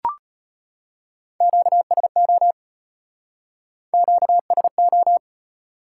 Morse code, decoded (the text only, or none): QSO QSO